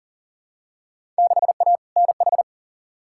Morse code, heard as a keyed tone, about 40 wpm, 700 Hz. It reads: 6A NH
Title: Field Day Exchanges - 40wpm